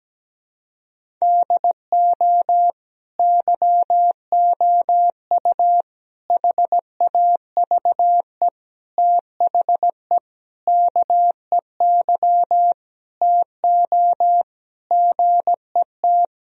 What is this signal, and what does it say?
Morse code, heard as a keyed tone, about 17 wpm, 700 Hz. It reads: DO YOU HAVE THE KEY TO GET
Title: Sentences from Top 1000 Words - Review Part 4 - 17wpm